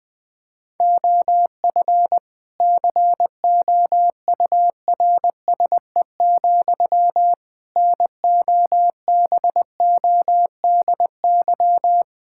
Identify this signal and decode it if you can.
Morse code, heard as a keyed tone, about 20 wpm, 700 Hz. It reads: OF COURSE, NOBODY
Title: Sentences from Top 1600 Words 20wpm